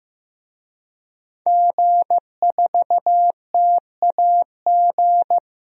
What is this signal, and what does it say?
Morse code, heard as a keyed tone, about 15 wpm, 700 Hz. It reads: G4TAG
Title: DXCC-100 Call Signs - 5 Characters - 15wpm